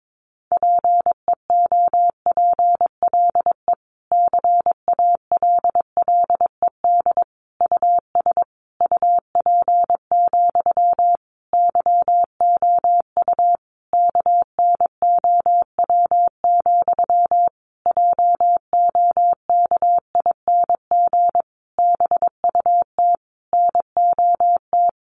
Morse code, heard as a keyed tone, about 22 wpm, 700 Hz. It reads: PEOPLE CALLED US UP, YOU KNOW, JOKING BUT NOT
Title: A Sound Of Thunder 22wpm